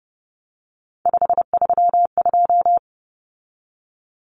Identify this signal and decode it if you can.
Morse code, heard as a keyed tone, about 30 wpm, 700 Hz. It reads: 532
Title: Common Cut Numbers - 3-Digit 30wpm